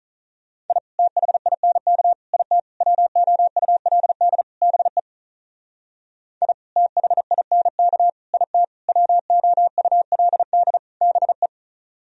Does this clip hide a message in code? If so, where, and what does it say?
Yes: Morse code from 0.7 to 11.5 s, heard as a keyed tone, about 46 wpm, 700 Hz. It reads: I THINK IT WOULD BE I THINK IT WOULD BE